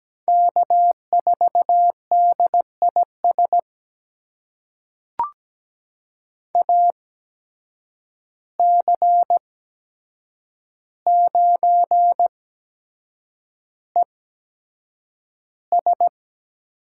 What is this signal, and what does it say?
Morse code, heard as a keyed tone, about 17 wpm, spaced as for 7 wpm, 700 Hz. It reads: K4DIS AC9ES
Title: USA Call Signs - 8x Character Spacing - 17wpm